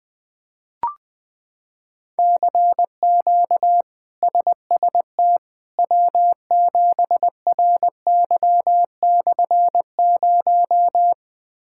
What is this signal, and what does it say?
Morse code, heard as a keyed tone, about 20 wpm, 700 Hz. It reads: CQ SST W7RY/0
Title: K1USN SST QSO Practice - 20wpm